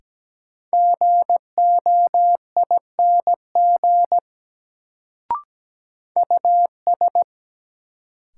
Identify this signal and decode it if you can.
Morse code, heard as a keyed tone, about 17 wpm, 700 Hz. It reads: GOING US
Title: Sets of 2 Words - New Word Drill - 17wpm